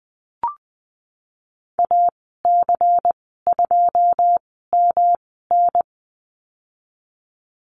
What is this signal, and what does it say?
Morse code, heard as a keyed tone, about 19 wpm, spaced as for 14 wpm, 700 Hz. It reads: AC2MN
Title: 161 Lesson 26g - Letter g - Call Signs 20wpm